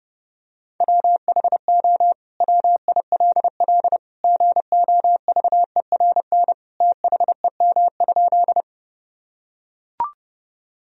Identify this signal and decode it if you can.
Morse code, heard as a keyed tone, about 30 wpm, 700 Hz. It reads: WHO WILL GOVERN THEM?
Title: Sentences from Top 1000 Words - Review Part 2 - 30wpm